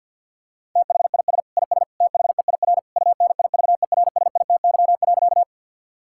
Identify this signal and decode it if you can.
Morse code, heard as a keyed tone, about 50 wpm, 700 Hz. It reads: THIS IS THEIR UNIVERSITY.
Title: Sentences from Top 1000 Words - Review Part 5 - 50wpm